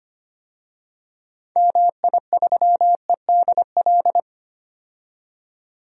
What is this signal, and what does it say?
Morse code, heard as a keyed tone, about 25 wpm, 700 Hz. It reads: MI3EDL
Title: DXCC-100 Call Signs - 6 Characters - 25wpm